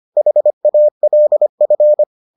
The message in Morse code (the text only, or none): HALF